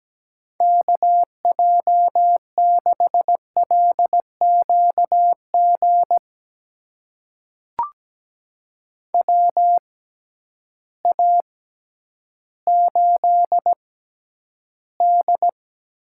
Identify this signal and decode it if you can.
Morse code, heard as a keyed tone, about 17 wpm, spaced as for 10 wpm, 700 Hz. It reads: KJ6LQG WA8D
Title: USA Call Signs - 6 Characters - 6x Character Spacing - 17wpm